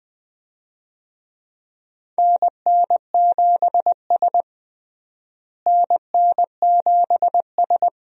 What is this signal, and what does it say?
Morse code, heard as a keyed tone, about 21 wpm, 700 Hz. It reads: NN7S NN7S
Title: USA Call Signs - 4 Characters - Repeated 2x - 20wpm